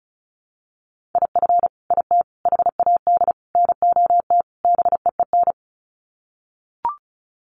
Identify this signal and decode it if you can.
Morse code, heard as a keyed tone, about 34 wpm, 700 Hz. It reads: IF IT HAD NOT BEEN
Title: Sets of 5 Words Encore 35wpm